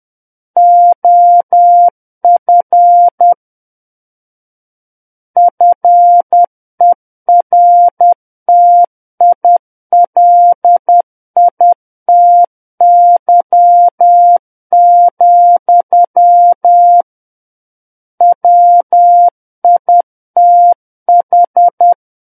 Morse code, heard as a keyed tone, about 10 wpm, 700 Hz. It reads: OF FERTILITY, WITH
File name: Book One / 01-Chapter 1 - 10 WPM